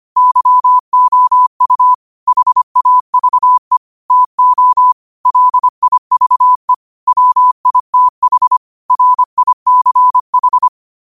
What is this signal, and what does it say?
Morse code, heard as a keyed tone, about 25 wpm, 1000 Hz. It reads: YOU HAVE TO LIVE WITH RICH
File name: qotdfiles / 25wpm